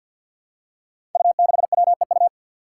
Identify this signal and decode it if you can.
Morse code, heard as a keyed tone, about 53 wpm, 700 Hz. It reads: U6WEU
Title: Five Challenging Letter-Number Combinations 50wpm